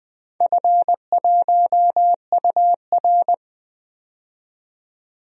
Morse code, heard as a keyed tone, about 20 wpm, 700 Hz. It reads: F1UR
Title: DXCC-100 Call Signs - 4 Characters - 20wpm